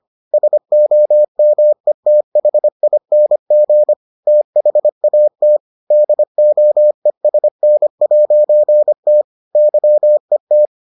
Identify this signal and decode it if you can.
Morse code, heard as a keyed tone, about 25 wpm, 600 Hz. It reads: SOMETHING THAT DOESN'T YET